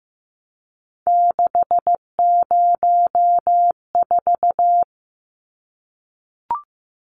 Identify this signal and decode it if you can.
Morse code, heard as a keyed tone, about 15 wpm, 700 Hz. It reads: 604
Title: Numbers - 3-Digit 15wpm